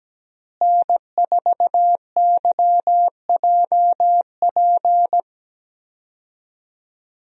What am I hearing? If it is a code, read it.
Morse code, heard as a keyed tone, about 17 wpm, 700 Hz. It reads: N4YJP
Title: USA Call Signs - 5 Characters - Repeated 3x - 17wpm